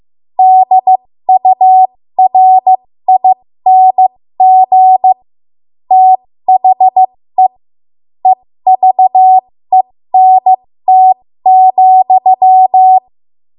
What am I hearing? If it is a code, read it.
Morse code, heard as a keyed tone, about 15 wpm, spaced as for 13 wpm, 750 Hz. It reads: DURING THE EVENT,
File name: Morse / 13WPM